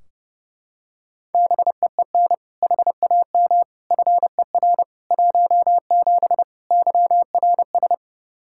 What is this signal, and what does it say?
Morse code, heard as a keyed tone, about 30 wpm, 700 Hz. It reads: BEEN HAM FER 17 YRS
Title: K1USN SST QSO Practice 30wpm